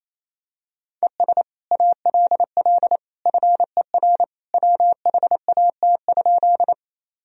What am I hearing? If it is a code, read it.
Morse code, heard as a keyed tone, about 28 wpm, 700 Hz. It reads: ES ALL FER WHAT?
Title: Sentences from Top 100 Words and Abbreviations 28wpm